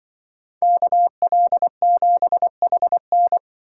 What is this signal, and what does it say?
Morse code, heard as a keyed tone, about 23 wpm, 700 Hz. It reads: KL7HN
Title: Speed Racing - USA 2x2 Call Signs 20wpm